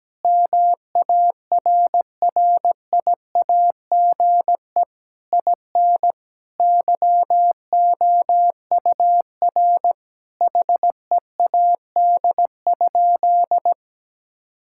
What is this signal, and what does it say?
Morse code, heard as a keyed tone, about 17 wpm, 700 Hz. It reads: MARRIAGE IN YOUR HEAD?
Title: Sentences from Top 1000 Words - Review Part 5 - 17wpm